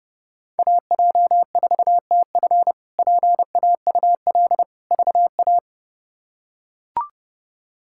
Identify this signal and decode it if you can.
Morse code, heard as a keyed tone, about 30 wpm, 700 Hz. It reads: AJ4TF PAUL VA
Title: K1USN SST QSO Practice - 30wpm